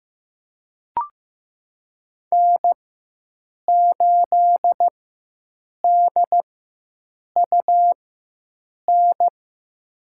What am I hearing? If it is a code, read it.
Morse code, heard as a keyed tone, about 15 wpm, spaced as for 7 wpm, 700 Hz. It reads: N8DUN
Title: USA Call Signs - 4x Character Spacing - 15wpm